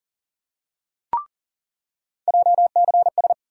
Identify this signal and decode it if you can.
Morse code, heard as a keyed tone, about 40 wpm, 700 Hz. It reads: JCS
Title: USA Call Sign Postfixes - 3 Letters 40wpm